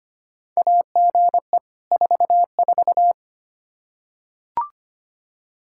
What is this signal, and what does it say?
Morse code, heard as a keyed tone, about 25 wpm, 700 Hz. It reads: AGE 44
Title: Practice QSO - Exchange 3 - More Info - 25wpm